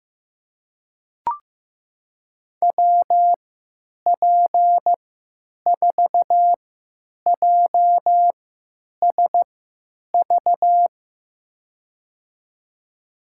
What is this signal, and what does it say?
Morse code, heard as a keyed tone, about 15 wpm, spaced as for 9 wpm, 700 Hz. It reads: WP4JSV
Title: USA Call Signs - 6 Characters - 3x Character Spacing - 15wpm